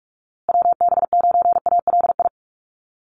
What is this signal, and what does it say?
Morse code, heard as a keyed tone, about 44 wpm, 700 Hz. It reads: WB9ALI
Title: USA Call Signs Repeated 3x 45wpm